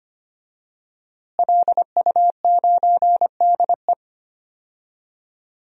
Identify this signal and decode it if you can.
Morse code, heard as a keyed tone, about 25 wpm, 700 Hz. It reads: LU9DE
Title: DXCC-100 Call Signs - 5 Characters - Repeated 3x - 25wpm